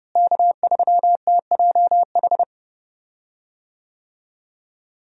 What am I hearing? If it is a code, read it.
Morse code, heard as a keyed tone, about 30 wpm, 700 Hz. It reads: K3TJH
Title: USA Call Signs - 5 Characters - Repeated 3x - 30wpm